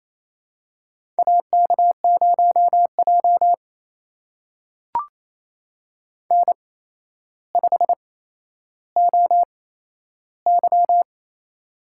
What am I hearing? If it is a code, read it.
Morse code, heard as a keyed tone, about 28 wpm, spaced as for 12 wpm, 700 Hz. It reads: AK0J N5OY
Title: USA Call Signs - 4 Characters - 8x Character Spacing - 28wpm